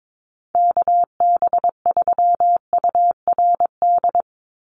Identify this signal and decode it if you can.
Morse code, heard as a keyed tone, about 22 wpm, 700 Hz. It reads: KB3URD
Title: USA Call Signs 22wpm